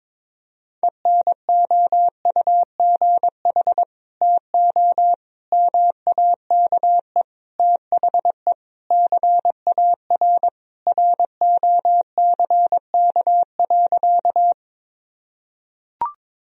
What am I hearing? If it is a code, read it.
Morse code, heard as a keyed tone, about 21 wpm, 700 Hz. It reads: ENOUGH TO MAKE THE CAR ROCK.
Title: Sentences from Top 1000 Words - Review Part 1 - 22wpm